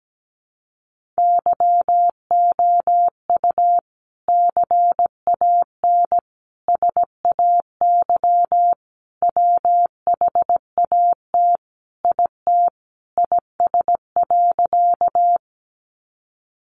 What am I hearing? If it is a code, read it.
Morse code, heard as a keyed tone, about 17 wpm, 700 Hz. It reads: YOU CAN SAY WHAT IT IS.
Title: Sentences from Top 200 Words 17wpm